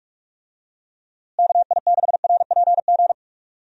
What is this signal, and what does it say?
Morse code, heard as a keyed tone, about 46 wpm, 700 Hz. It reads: KI6RPG
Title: USA Call Signs - 6 Characters - Repeated 2x - 45wpm